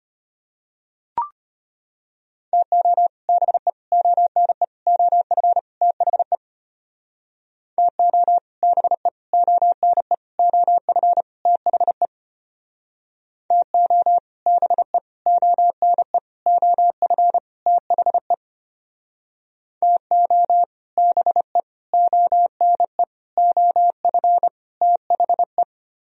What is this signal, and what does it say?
Morse code, heard as a keyed tone, about 32 wpm, 700 Hz. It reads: TO BE ONE OF THE TO BE ONE OF THE TO BE ONE OF THE TO BE ONE OF THE